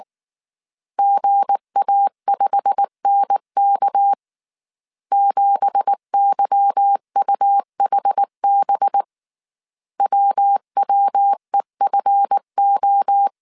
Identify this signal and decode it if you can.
Morse code, heard as a keyed tone, about 19 wpm, 800 Hz. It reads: GA5NK 7YUHB WWEFO
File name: CW / DET / 1 9 1500C S A 18wpm 2459